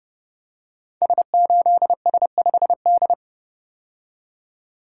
Morse code, heard as a keyed tone, about 30 wpm, 700 Hz. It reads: S8S5D
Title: Code Groups - 5 Letter:Numbers - 30wpm